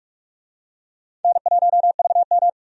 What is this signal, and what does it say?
Morse code, heard as a keyed tone, about 47 wpm, 700 Hz. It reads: N1VM
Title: USA Call Signs - 4 Characters - 45wpm